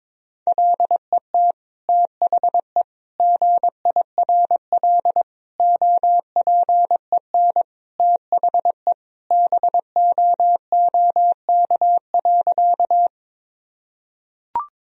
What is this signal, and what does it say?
Morse code, heard as a keyed tone, about 22 wpm, 700 Hz. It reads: LET THE GIRL OPEN THE BOOK.
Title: Sentences from Top 1000 Words - Review Part 1 - 22wpm